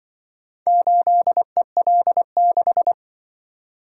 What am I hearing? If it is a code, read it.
Morse code, heard as a keyed tone, about 25 wpm, 700 Hz. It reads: 8EL6
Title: Speed-Racing - Four Letter-Number Combinations - 20wpm